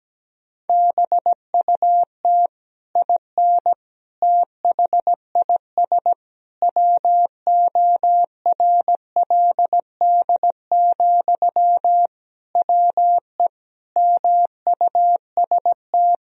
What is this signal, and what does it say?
Morse code, heard as a keyed tone, about 17 wpm, 700 Hz. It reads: BUT IN THIS WORLD, WE MUST
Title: Sentences from Top 1000 Words - Review Part 4 - 17wpm